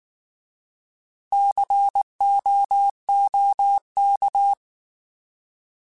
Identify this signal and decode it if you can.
Morse code, heard as a keyed tone, about 19 wpm, 800 Hz. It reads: COOK
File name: Alphabetical / Alphabetical mp3s / Cs @18